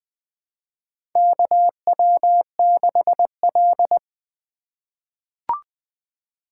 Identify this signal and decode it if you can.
Morse code, heard as a keyed tone, about 20 wpm, 700 Hz. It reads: KW6L
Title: USA Call Signs - 2x Character Spacing - 20wpm